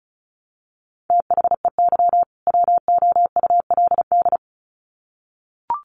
Morse code, heard as a keyed tone, about 34 wpm, 700 Hz. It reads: THEY WOULD
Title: Sets of 2 Words 35wpm